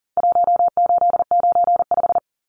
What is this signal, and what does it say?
Morse code, heard as a keyed tone, about 40 wpm, 700 Hz. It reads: 1895
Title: Numbers - Contest Style - 4-Digit 40wpm